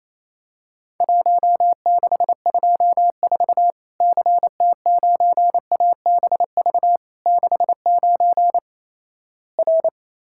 Morse code, heard as a keyed tone, about 27 wpm, 700 Hz. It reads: 1624 CT9ABV 69 R
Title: Worked All Europe DX Contest - QTC Exchange - 28wpm